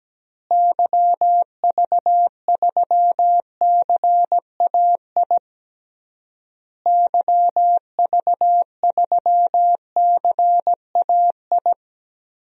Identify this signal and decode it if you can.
Morse code, heard as a keyed tone, about 17 wpm, 700 Hz. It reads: YV3CAI YV3CAI
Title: DXCC-100 Call Signs - 6 Characters - Repeated 3x - 17wpm